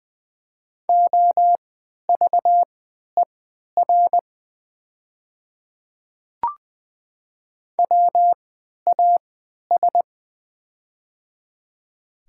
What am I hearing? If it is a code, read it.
Morse code, heard as a keyed tone, about 20 wpm, spaced as for 11 wpm, 700 Hz. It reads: OVER WAS
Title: 3x Character Spacing - Top 100 Words - 20wpm